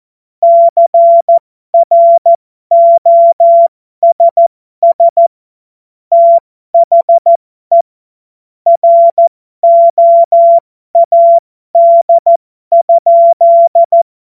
Morse code, heard as a keyed tone, about 14 wpm, spaced as for 12 wpm, 680 Hz. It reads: CROSS THE ROAD?
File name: Morsekurs / Witze